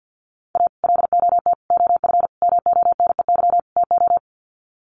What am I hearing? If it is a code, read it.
Morse code, heard as a keyed tone, about 50 wpm, 700 Hz. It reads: A LOT OF MONEY TO
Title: Sets of 5 Words Encore2 50wpm